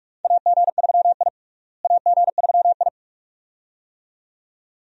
Morse code, heard as a keyed tone, about 42 wpm, 700 Hz. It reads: AG3I AG3I
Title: USA Call Signs - 4 Characters - Repeated 2x - 45wpm